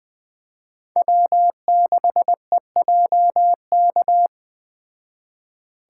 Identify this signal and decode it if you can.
Morse code, heard as a keyed tone, about 20 wpm, 700 Hz. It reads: W6EJK
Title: Speed Racing - USA 1x3 Call Signs 17wpm